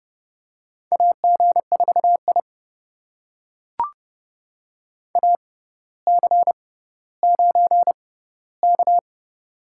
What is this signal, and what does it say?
Morse code, heard as a keyed tone, about 32 wpm, spaced as for 16 wpm, 700 Hz. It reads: AG4I AC9K
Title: USA Call Signs - 4 Characters - 6x Character Spacing - 30wpm